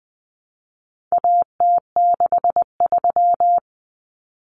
Morse code, heard as a keyed tone, about 20 wpm, 700 Hz. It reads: AT63
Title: Common Cut Numbers - 4-Digit 20wpm